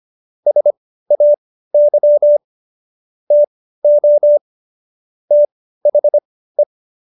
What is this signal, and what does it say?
Morse code, heard as a keyed tone, about 25 wpm, spaced as for 15 wpm, 590 Hz. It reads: SAY TO THE
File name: Jokes-25-char-15-eff